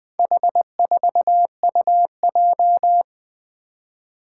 Morse code, heard as a keyed tone, about 20 wpm, 700 Hz. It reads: H4UJ
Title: Four Challenging Letter-Number Combinations 20wpm